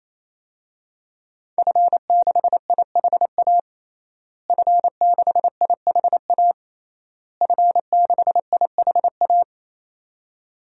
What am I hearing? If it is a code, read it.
Morse code, heard as a keyed tone, about 28 wpm, 700 Hz. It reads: F6IHA F6IHA F6IHA
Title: DXCC-100 Call Signs - 5 Characters - Repeated 3x - 28wpm